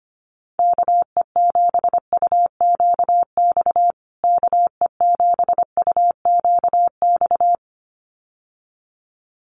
Morse code, heard as a keyed tone, about 25 wpm, 700 Hz. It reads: KE7UQX KE7UQX
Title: USA Call Signs Repeated 2x 25wpm